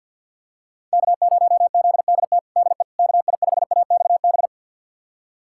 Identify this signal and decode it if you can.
Morse code, heard as a keyed tone, about 51 wpm, 700 Hz. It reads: K0ZDT DE KI5AXB
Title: USA Callsign DE Callsign 50wpm